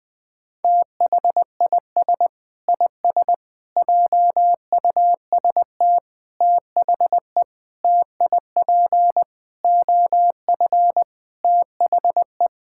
Morse code, heard as a keyed tone, about 20 wpm, 700 Hz. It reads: THIS IS JUST THE TIP OF THE
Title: Sets of 8 Words - 20wpm